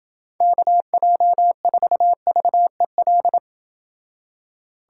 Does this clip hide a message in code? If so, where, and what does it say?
Yes: Morse code from 0.4 to 3.4 s, heard as a keyed tone, about 29 wpm, 700 Hz. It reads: KJ4VEL